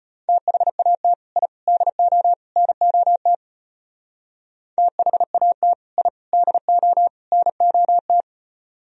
Morse code, heard as a keyed tone, about 32 wpm, 700 Hz. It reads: THAT I DO NOT THAT I DO NOT